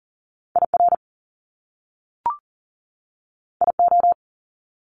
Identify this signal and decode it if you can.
Morse code, heard as a keyed tone, about 38 wpm, 700 Hz. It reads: IR IO
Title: Top Two Letter Combinations in Words 40wpm